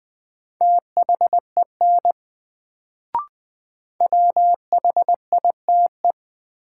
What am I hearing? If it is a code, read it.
Morse code, heard as a keyed tone, about 20 wpm, 700 Hz. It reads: THEN WHITE
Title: Sets of 2 Words - New Word Drill - 20wpm